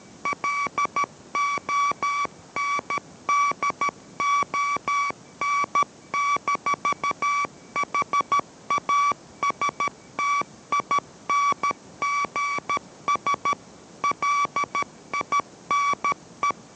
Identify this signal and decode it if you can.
Morse code, heard as a keyed tone, about 13 wpm, 1140 Hz. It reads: LONDON-HASTINGS LINE